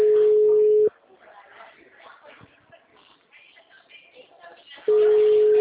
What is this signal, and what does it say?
Ringback tone (Europe, 425 Hz), with speech.
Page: Araras 4